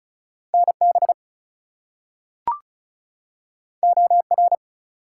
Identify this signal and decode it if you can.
Morse code, heard as a keyed tone, about 36 wpm, 700 Hz. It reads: NB OR
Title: US States and Canadian Provinces 35wpm